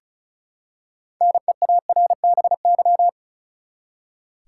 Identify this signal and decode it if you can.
Morse code, heard as a keyed tone, about 36 wpm, 700 Hz. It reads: NEARBY
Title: Six Letter Words - 35wpm